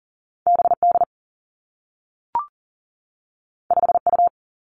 Morse code, heard as a keyed tone, about 40 wpm, 700 Hz. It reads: B D 5 U
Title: Two Challenging Letter-Number Combinations 40wpm